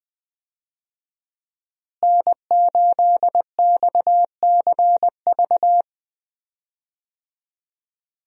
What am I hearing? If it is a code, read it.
Morse code, heard as a keyed tone, about 20 wpm, 700 Hz. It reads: N8XCV
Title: USA Call Signs - 5 Characters - Repeated 3x - 20wpm